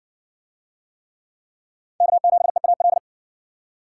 Morse code, heard as a keyed tone, about 60 wpm, 700 Hz. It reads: K7EAL